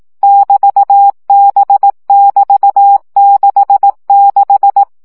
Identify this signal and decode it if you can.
Morse code, heard as a keyed tone, about 18 wpm, 800 Hz. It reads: <BT>B<BT>66
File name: CW gyakorlas b 6 egyenlo minusz 15 baud